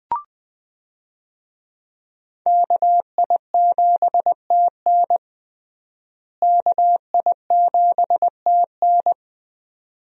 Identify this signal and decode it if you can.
Morse code, heard as a keyed tone, about 20 wpm, 700 Hz. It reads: KI7TN KI7TN
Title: USA Call Signs - 5 Characters - Repeated 2x - 20wpm